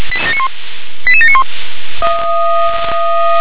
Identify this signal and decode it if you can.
Five-tone selective call (ZVEI), heard as quick runs of five tones, tones of about 80 ms. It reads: 88791